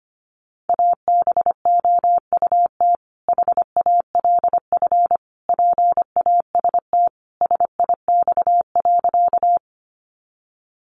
Morse code, heard as a keyed tone, about 25 wpm, 700 Hz. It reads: ABOUT HALF PAST SIX.
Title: Sentences from Top 600 Words 25wpm